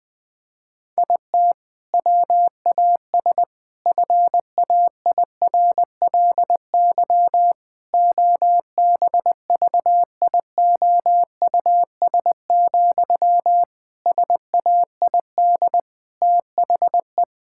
Morse code, heard as a keyed tone, about 20 wpm, 700 Hz. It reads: IT WAS FAIRLY OBVIOUS, SAID THE
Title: Sentences from Top 1600 Words 20wpm